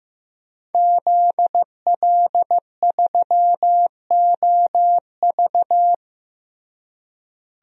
Morse code, heard as a keyed tone, about 16 wpm, 700 Hz. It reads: ZL3OV
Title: DXCC-100 Call Signs - 5 Characters - Repeated 3x - 15wpm